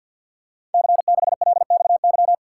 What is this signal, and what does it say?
Morse code, heard as a keyed tone, about 52 wpm, 700 Hz. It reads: C6LXY
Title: Code Groups - 5 Letter:Numbers - 50wpm